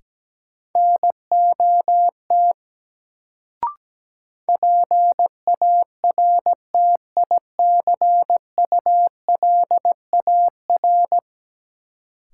Morse code, heard as a keyed tone, about 17 wpm, 700 Hz. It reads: NOT PARTICULAR
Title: Sentences from Top 600 Words - New Word Drill - 17wpm